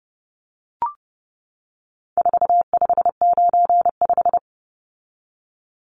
Morse code, heard as a keyed tone, about 30 wpm, 700 Hz. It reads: 4595
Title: Numbers - 4-Digit 30wpm